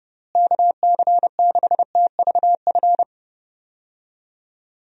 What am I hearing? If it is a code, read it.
Morse code, heard as a keyed tone, about 30 wpm, 700 Hz. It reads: KC6TVF
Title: USA Call Signs - 6 Characters - Repeated 2x - 30wpm